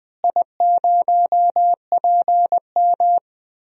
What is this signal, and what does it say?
Morse code, heard as a keyed tone, about 20 wpm, 700 Hz. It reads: I0PM
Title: DXCC-100 Call Signs - 4 Characters - Repeated 2x - 20wpm